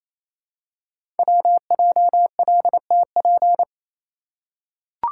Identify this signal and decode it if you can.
Morse code, heard as a keyed tone, about 29 wpm, 700 Hz. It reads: WJLTP
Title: Code Groups - 5 Letters - 28wpm